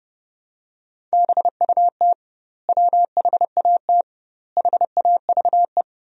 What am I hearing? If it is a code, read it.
Morse code, heard as a keyed tone, about 32 wpm, 700 Hz. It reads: BUT WHAT HAVE
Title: 2x Word-Spacing - Sentences from Top 100 Words - 30wpm